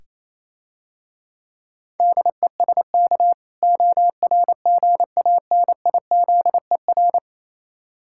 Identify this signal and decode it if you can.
Morse code, heard as a keyed tone, about 27 wpm, 700 Hz. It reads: DESK ORGANIZER
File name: Easy Words - Speed-Racing with Hints - 22wpm